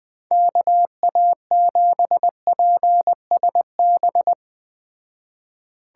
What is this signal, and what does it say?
Morse code, heard as a keyed tone, about 20 wpm, 700 Hz. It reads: KA7PSB